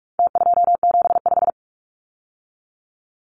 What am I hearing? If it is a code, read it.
Morse code, heard as a keyed tone, about 44 wpm, 700 Hz. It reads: T275
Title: Numbers - Contest Style - 4-Digit 45wpm